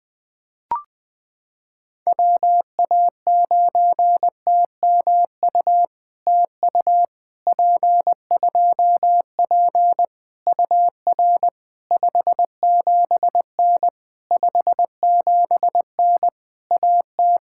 Morse code, heard as a keyed tone, about 20 wpm, 700 Hz. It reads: WA9TMU TU P2P UR 57N 57N AT
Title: POTA QSO Practice - 20wpm